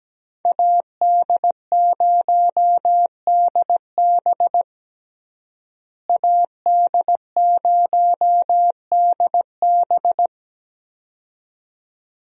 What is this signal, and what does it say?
Morse code, heard as a keyed tone, about 17 wpm, 700 Hz. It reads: AD0DB AD0DB
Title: USA Call Signs - 5 Characters - Repeated 2x - 17wpm